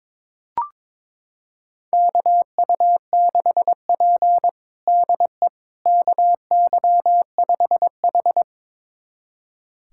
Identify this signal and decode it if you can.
Morse code, heard as a keyed tone, about 22 wpm, 700 Hz. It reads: KU6P DE KY5H
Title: USA Callsign DE Callsign - 4 Characters - 22wpm